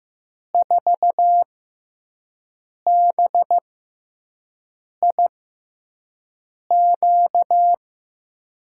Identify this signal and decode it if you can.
Morse code, heard as a keyed tone, about 15 wpm, spaced as for 5 wpm, 700 Hz. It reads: 4BIQ
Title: USA Call Signs - 5 Characters - 6x Character Spacing - 15wpm